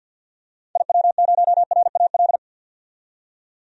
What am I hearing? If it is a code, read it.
Morse code, heard as a keyed tone, about 51 wpm, 700 Hz. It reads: IW0RAL
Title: DXCC-100 Call Signs - 6 Characters - 50wpm